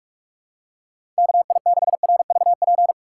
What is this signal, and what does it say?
Morse code, heard as a keyed tone, about 46 wpm, 700 Hz. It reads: KI6RVP
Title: USA Call Signs - 3x Character Spacing - 45wpm